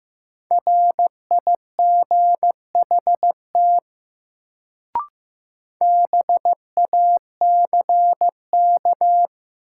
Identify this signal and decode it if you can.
Morse code, heard as a keyed tone, about 15 wpm, 700 Hz. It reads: RIGHT BACK
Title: Sentences from Top 200 Words - New Word Drill - 15wpm